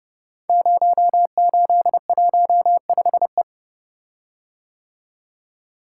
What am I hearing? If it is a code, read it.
Morse code, heard as a keyed tone, about 30 wpm, 700 Hz. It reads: 0815E
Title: Code Groups - 5 Letter:Numbers - 30wpm